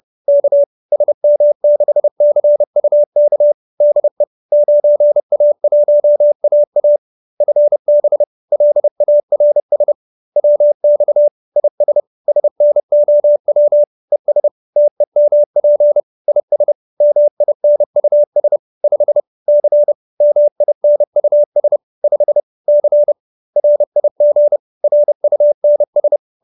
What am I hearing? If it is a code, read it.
Morse code, heard as a keyed tone, about 30 wpm, 590 Hz. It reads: K SM6CUK DE 9A1AA FB LARS WX IS SNOW ES TEMP IS MINUS 5 C MINUS 5 C RIG RUNS